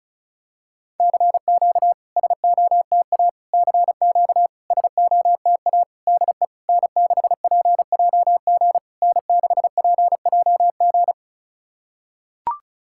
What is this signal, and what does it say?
Morse code, heard as a keyed tone, about 32 wpm, 700 Hz. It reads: CQ SOTA CQ SOTA DE N6PJG N6PJG
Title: SOTA QSO Practice 35wpm